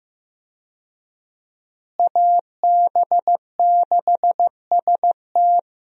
Morse code, heard as a keyed tone, about 15 wpm, 700 Hz. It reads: AB6ST
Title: USA Call Signs - 5 Characters - Repeated 3x - 15wpm